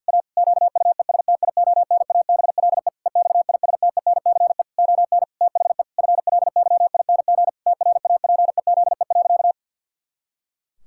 Morse code, heard as a keyed tone, about 52 wpm, 700 Hz. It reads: A QUESTIONABLE EXISTENCE ON THE FLYING TRAPEZE.
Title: Pangram Challenge v2 - 50wpm